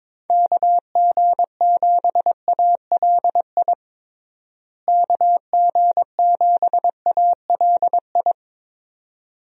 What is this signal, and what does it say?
Morse code, heard as a keyed tone, about 22 wpm, 700 Hz. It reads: KG7ALI KG7ALI
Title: USA Call Signs - 6 Characters - Repeated 2x - 22wpm